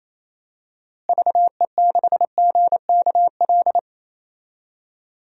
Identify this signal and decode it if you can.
Morse code, heard as a keyed tone, about 28 wpm, 700 Hz. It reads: VE6GKL
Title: DXCC-100 Call Signs - 6 Characters - Repeated 2x - 28wpm